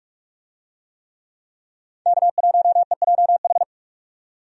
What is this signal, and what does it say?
Morse code, heard as a keyed tone, about 46 wpm, 700 Hz. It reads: K1EJH
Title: USA Call Signs - 5 Characters - 45wpm